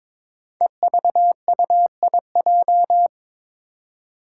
Morse code, heard as a keyed tone, about 22 wpm, 700 Hz. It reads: EVUIJ
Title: Five Challenging Letter-Number Combinations 22wpm